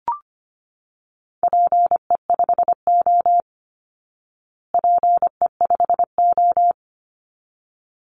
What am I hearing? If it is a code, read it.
Morse code, heard as a keyed tone, about 25 wpm, 700 Hz. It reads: PE5O PE5O
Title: Call Signs Repeated 3x - 25wpm